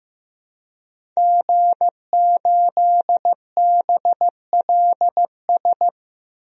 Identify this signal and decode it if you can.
Morse code, heard as a keyed tone, about 15 wpm, 700 Hz. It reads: G8BLS
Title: DXCC-100 Call Signs - 5 Characters - Repeated 2x - 15wpm